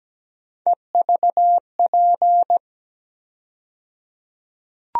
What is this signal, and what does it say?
Morse code, heard as a keyed tone, about 17 wpm, 700 Hz. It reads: EVP
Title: USA Call Sign Postfixes - 3 Letters 17wpm